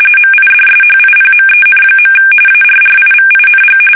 RTTY, about 45.45 baud. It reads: MIDSHIPMEN. ARRIVING
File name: fsk850 / battleshipnj